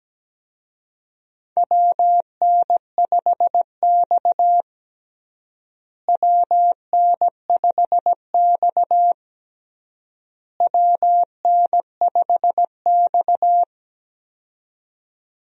Morse code, heard as a keyed tone, about 17 wpm, 700 Hz. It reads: WN5X WN5X WN5X
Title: USA Call Signs - 4 Characters - Repeated 3x - 17wpm